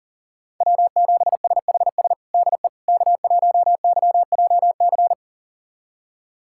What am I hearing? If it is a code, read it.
Morse code, heard as a keyed tone, about 40 wpm, 700 Hz. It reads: W7SHS DE K1YJC
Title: USA Callsign DE Callsign - 5 Characters - 40wpm